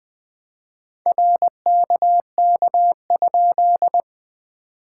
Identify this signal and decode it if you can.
Morse code, heard as a keyed tone, about 20 wpm, 660 Hz. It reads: RKK?
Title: POTA QSO Practice - 20wpm